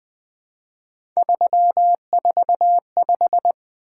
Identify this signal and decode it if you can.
Morse code, heard as a keyed tone, about 20 wpm, 700 Hz. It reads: 345
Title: Three Letter-Number Combinations - 20wpm